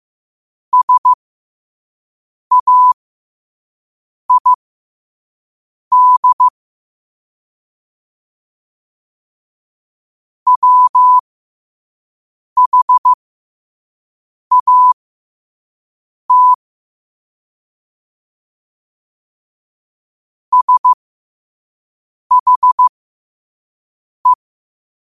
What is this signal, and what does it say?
Morse code, heard as a keyed tone, about 15 wpm, spaced as for 5 wpm, 1000 Hz. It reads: SAID WHAT SHE
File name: qotdfiles / 5wpm